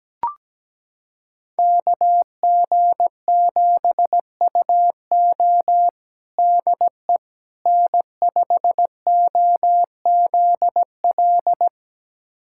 Morse code, heard as a keyed tone, about 17 wpm, 700 Hz. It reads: KG7UO DE N5OZL
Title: USA Callsign DE Callsign - 5 Characters - 17wpm